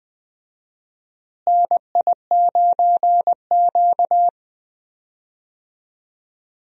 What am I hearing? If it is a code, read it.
Morse code, heard as a keyed tone, about 20 wpm, 700 Hz. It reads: NI9Q
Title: USA Call Signs - 4 Characters - Repeated 2x - 20wpm